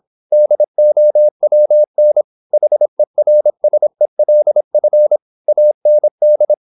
Morse code, heard as a keyed tone, about 26 wpm, 600 Hz. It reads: DOWN HERSELF AND
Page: Never 26wpm | CW med Gnister